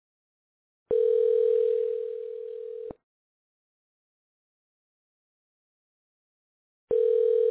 Ringback tone (North America), 440 + 480 Hz.